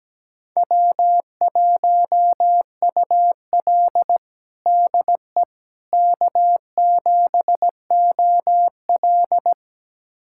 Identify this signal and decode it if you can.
Morse code, heard as a keyed tone, about 17 wpm, 700 Hz. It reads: W1UL DE K7OL
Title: USA Callsign DE Callsign - 4 Characters - 17wpm